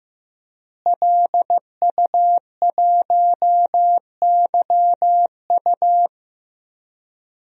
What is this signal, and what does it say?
Morse code, heard as a keyed tone, about 15 wpm, 700 Hz. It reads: LU1YU
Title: DXCC-100 Call Signs - 5 Characters - Repeated 2x - 15wpm